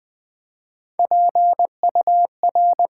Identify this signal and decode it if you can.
Morse code, heard as a keyed tone, about 20 wpm, 700 Hz. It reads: PUR
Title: Build A Word - 3 Letters At A Time - 6-Letter Words - 20wpm